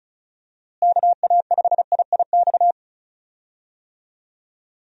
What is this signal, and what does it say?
Morse code, heard as a keyed tone, about 34 wpm, 700 Hz. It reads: KA5IIX
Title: USA Call Signs - 6 Characters - Repeated 3x - 35wpm